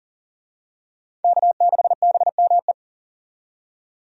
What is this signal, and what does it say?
Morse code, heard as a keyed tone, about 37 wpm, 700 Hz. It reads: K6BME
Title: USA Call Signs - 5 Characters - 40wpm